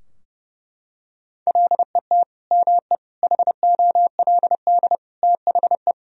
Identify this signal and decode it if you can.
Morse code, heard as a keyed tone, about 30 wpm, 700 Hz. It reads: LET ME HOLD THE
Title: Sentences from Top 1000 Words - Review Part 2 - 30wpm